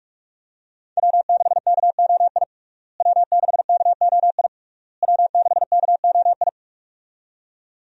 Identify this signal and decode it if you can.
Morse code, heard as a keyed tone, about 45 wpm, 700 Hz. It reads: W6KOI W6KOI W6KOI
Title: USA Call Signs - 5 Characters - Repeated 3x - 45wpm